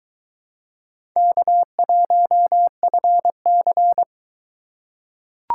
Morse code, heard as a keyed tone, about 22 wpm, 700 Hz. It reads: K1FC